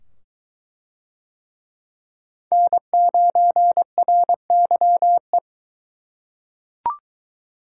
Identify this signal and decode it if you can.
Morse code, heard as a keyed tone, about 21 wpm, 700 Hz. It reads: N9RYE